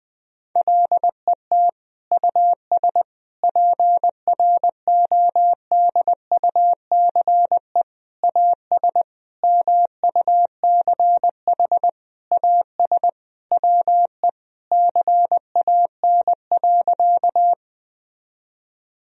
Morse code, heard as a keyed tone, about 21 wpm, 700 Hz. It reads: LET US PRODUCE AS MUCH AS WE CAN.
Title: Sentences from Top 1000 Words - Review Part 3 - 20wpm